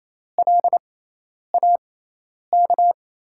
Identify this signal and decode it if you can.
Morse code, heard as a keyed tone, about 28 wpm, spaced as for 10 wpm, 700 Hz. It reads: LAK
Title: USA Call Signs - 5 Characters - 6x Character Spacing - 28wpm